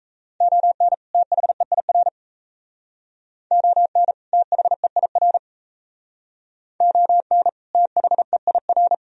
Morse code, heard as a keyed tone, about 35 wpm, 700 Hz. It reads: ON THEIR ON THEIR ON THEIR